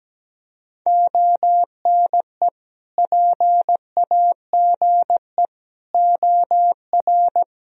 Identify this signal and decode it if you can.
Morse code, heard as a keyed tone, about 17 wpm, 700 Hz. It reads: ONE PAGE OR
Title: Sentences from Top 1000 Words - Review Part 1 - 17wpm